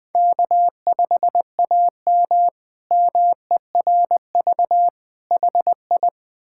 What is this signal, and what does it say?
Morse code, heard as a keyed tone, about 21 wpm, 700 Hz. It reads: K5AM MERV HI
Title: K1USN SST QSO Practice - 20wpm